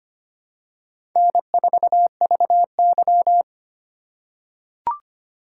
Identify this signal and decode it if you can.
Morse code, heard as a keyed tone, about 25 wpm, 700 Hz. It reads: N4VY
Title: USA Call Signs - 4 Characters - 8x Character Spacing - 25wpm